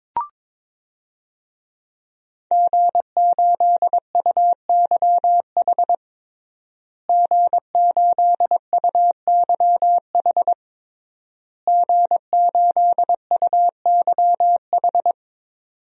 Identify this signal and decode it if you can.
Morse code, heard as a keyed tone, about 22 wpm, 700 Hz. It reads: G8UYH G8UYH G8UYH
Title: DXCC-100 Call Signs - 5 Characters - Repeated 3x - 22wpm